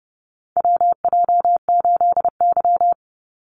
Morse code, heard as a keyed tone, about 30 wpm, 700 Hz. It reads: WJ8Y
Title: Call Signs Repeated 2x - 30wpm